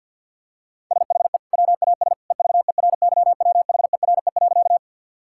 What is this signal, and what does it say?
Morse code, heard as a keyed tone, about 50 wpm, 700 Hz. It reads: SHE WAS EVERYWHERE.
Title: Sentences from Top 1600 Words 50wpm